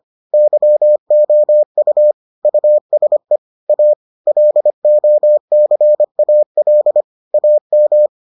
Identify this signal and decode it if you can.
Morse code, heard as a keyed tone, about 25 wpm, 600 Hz. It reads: YOU USE A LOCAL AM